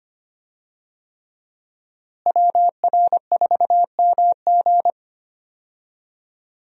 Morse code, heard as a keyed tone, about 25 wpm, 700 Hz. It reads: WR4MG
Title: USA Call Signs - 5 Characters - Repeated 3x - 25wpm